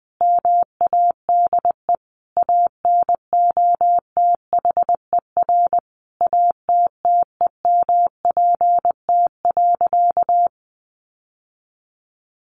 Morse code, heard as a keyed tone, about 20 wpm, 700 Hz. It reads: MADE ANOTHER ATTEMPT.
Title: Sentences from Top 1000 Words 20wpm